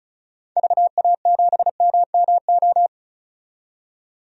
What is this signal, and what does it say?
Morse code, heard as a keyed tone, about 36 wpm, 700 Hz. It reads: VA7MMO
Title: DXCC-100 Call Signs - 6 Characters - 35wpm